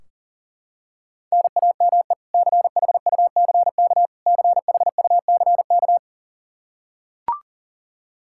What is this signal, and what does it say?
Morse code, heard as a keyed tone, about 40 wpm, 700 Hz. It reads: NAME CHUCK CHUCK
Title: K1USN SST QSO Practice 40wpm